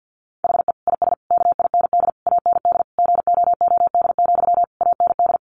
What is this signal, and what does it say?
Morse code, heard as a keyed tone, about 50 wpm, 700 Hz. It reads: HE IS KIND AND GOOD, AND
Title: Sentences from Top 1400 Words 50wpm